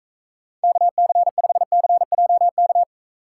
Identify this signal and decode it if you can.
Morse code, heard as a keyed tone, about 44 wpm, 700 Hz. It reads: KC5CJK